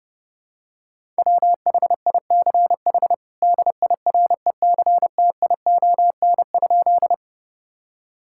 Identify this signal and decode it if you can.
Morse code, heard as a keyed tone, about 30 wpm, 700 Hz. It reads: WHICH DIRECTION?
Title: Sentences from Top 1000 Words - Review Part 5 - 30wpm